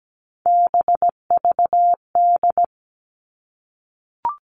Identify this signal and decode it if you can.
Morse code, heard as a keyed tone, about 17 wpm, 700 Hz. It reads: BVD
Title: All Cut Numbers - 3-Digit 17wpm